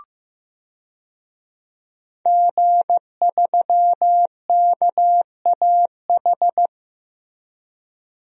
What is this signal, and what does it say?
Morse code, heard as a keyed tone, about 15 wpm, 700 Hz. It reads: G3KAH
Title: DXCC-100 Call Signs - 5 Characters - Repeated 2x - 15wpm